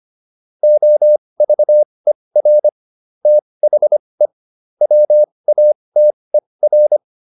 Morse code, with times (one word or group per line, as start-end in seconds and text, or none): 0.6-2.7 OVER
3.3-4.3 THE
4.8-7.0 WATER